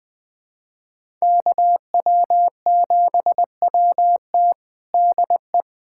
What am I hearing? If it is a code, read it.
Morse code, heard as a keyed tone, about 20 wpm, 700 Hz. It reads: KW7WT DE
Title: USA Callsign DE Callsign - 5 Characters - 20wpm